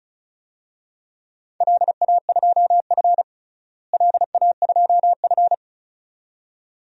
Morse code, heard as a keyed tone, about 34 wpm, 700 Hz. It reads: LA2F LA2F
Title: DXCC-100 Call Signs - 4 Characters - Repeated 2x - 35wpm